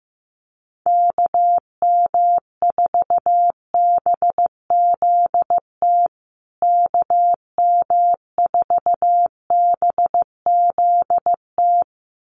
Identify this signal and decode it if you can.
Morse code, heard as a keyed tone, about 15 wpm, 700 Hz. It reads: KM4BZT KM4BZT
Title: USA Call Signs Repeated 2x 15wpm